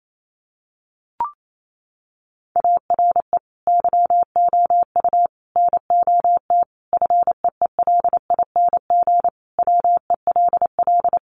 Morse code, heard as a keyed tone, about 28 wpm, 700 Hz. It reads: ARE YOU NOT FEELING WELL
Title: Sentences from Top 1200 Words 28wpm